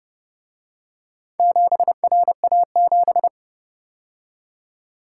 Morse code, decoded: 7RA7

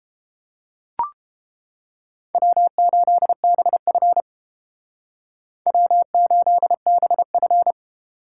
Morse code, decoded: W8BF W8BF

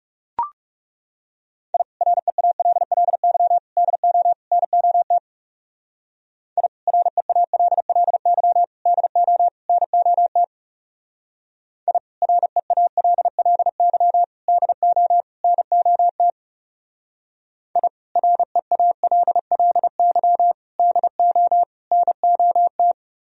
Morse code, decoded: I REALLY DO NOT I REALLY DO NOT I REALLY DO NOT I REALLY DO NOT

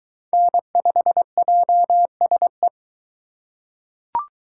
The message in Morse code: N5JSE

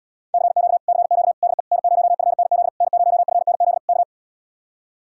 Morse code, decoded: CQ CQ DE N0FNZ N0FNZ K

